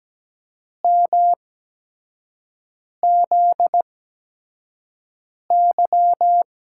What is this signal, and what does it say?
Morse code, heard as a keyed tone, about 17 wpm, spaced as for 5 wpm, 700 Hz. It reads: MZY
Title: USA Call Signs - 8x Character Spacing - 17wpm